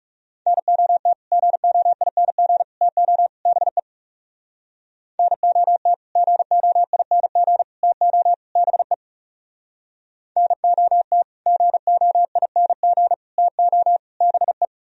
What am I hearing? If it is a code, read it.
Morse code, heard as a keyed tone, about 40 wpm, 700 Hz. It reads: NOT GOING TO BE NOT GOING TO BE NOT GOING TO BE